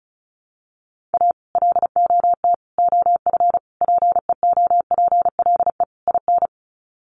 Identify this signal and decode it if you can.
Morse code, heard as a keyed tone, about 34 wpm, 700 Hz. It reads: A LOT OF PEOPLE IN
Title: Sets of 5 Words 35wpm